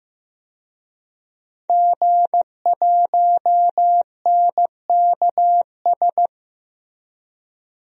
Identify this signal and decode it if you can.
Morse code, heard as a keyed tone, about 15 wpm, 700 Hz. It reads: G1NKS
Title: DXCC-100 Call Signs - 5 Characters - Repeated 3x - 15wpm